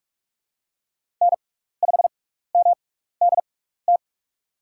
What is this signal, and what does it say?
Morse code, heard as a keyed tone, about 46 wpm, spaced as for 15 wpm, 700 Hz. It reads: N5MDT
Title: USA Call Signs - 6x Character Spacing - 45wpm